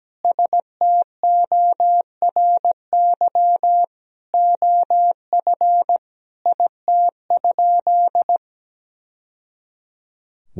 Morse code, heard as a keyed tone, about 17 wpm, 700 Hz. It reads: STORY OF IT?
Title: Sentences from Top 1000 Words - Review Part 1 - 17wpm